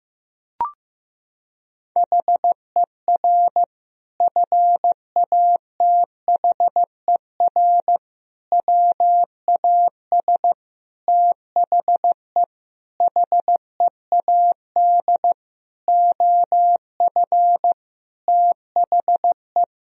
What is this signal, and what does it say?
Morse code, heard as a keyed tone, about 15 wpm, 700 Hz. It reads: HER FATHER WAS THE HEAD OF THE
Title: Sentences from Top 1000 Words - Review Part 4 - 15wpm